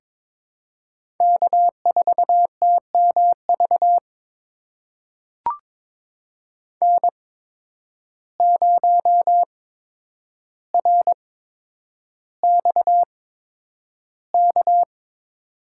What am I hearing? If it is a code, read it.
Morse code, heard as a keyed tone, about 23 wpm, spaced as for 10 wpm, 700 Hz. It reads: K4TMV N0RXK
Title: USA Call Signs - 5 Characters - 8x Character Spacing - 22wpm